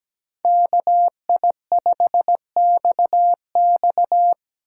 Morse code, heard as a keyed tone, about 19 wpm, 700 Hz. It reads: KI5XX
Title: Speed Racing - USA 2x2 Call Signs 17wpm